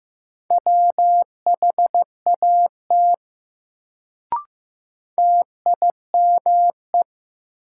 Morse code, heard as a keyed tone, about 15 wpm, 700 Hz. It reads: WHAT TIME
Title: Sets of 2 Words - New Word Drill - 15wpm